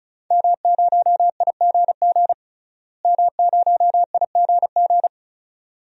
Morse code, heard as a keyed tone, about 36 wpm, 700 Hz. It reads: M0IGG M0IGG
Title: DXCC-100 Call Signs - 5 Characters - Repeated 2x - 35wpm